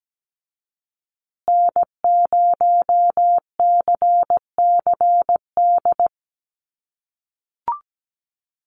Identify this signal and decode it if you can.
Morse code, heard as a keyed tone, about 17 wpm, 700 Hz. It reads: N0CCD
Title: USA Call Signs Repeated 3x 17wpm